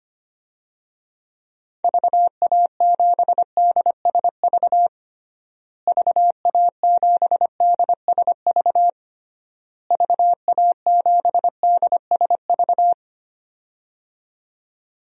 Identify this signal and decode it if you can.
Morse code, heard as a keyed tone, about 25 wpm, 700 Hz. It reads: VA7DSV VA7DSV VA7DSV
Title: DXCC-100 Call Signs - 6 Characters - Repeated 3x - 25wpm